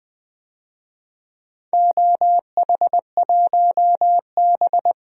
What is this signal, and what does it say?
Morse code, heard as a keyed tone, about 20 wpm, 700 Hz. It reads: OH1B
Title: DXCC-100 Call Signs - 4 Characters - Repeated 3x - 20wpm